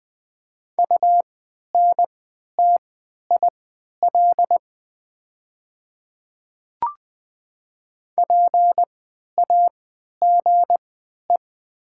Morse code, heard as a keyed tone, about 20 wpm, spaced as for 11 wpm, 700 Hz. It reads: UNTIL PAGE
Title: 3x Character Spacing - Top 300 Words - 20wpm